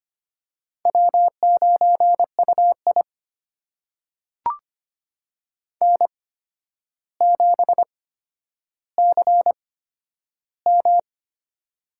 Morse code, heard as a keyed tone, about 25 wpm, spaced as for 11 wpm, 700 Hz. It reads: W9UI N7CM
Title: USA Call Signs - 4 Characters - 8x Character Spacing - 25wpm